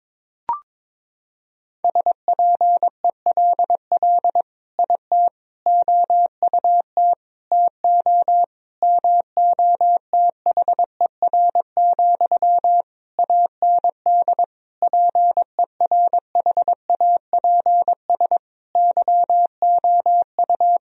Morse code, heard as a keyed tone, about 22 wpm, 700 Hz. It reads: SPELL IT OUT TO MOTHER, AND PERHAPS YOU
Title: Sentences from Top 1000 Words - Review Part 1 - 22wpm